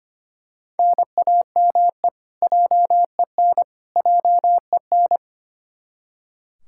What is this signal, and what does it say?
Morse code, heard as a keyed tone, about 25 wpm, 700 Hz. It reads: NAME JEN JEN
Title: Practice QSO - Exchange 1 - Beginning - 25wpm